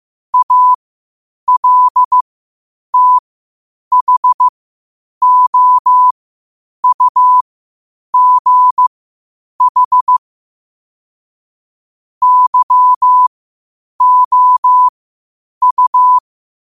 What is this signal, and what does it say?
Morse code, heard as a keyed tone, about 15 wpm, spaced as for 8 wpm, 1000 Hz. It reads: ALTHOUGH YOU
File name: qotdfiles / 8wpm